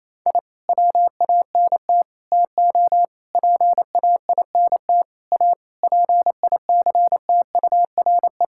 Morse code, heard as a keyed tone, about 28 wpm, 700 Hz. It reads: I WANT TO PAINT A PICTURE
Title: Sentences from Top 1000 Words - Review Part 6 - 28wpm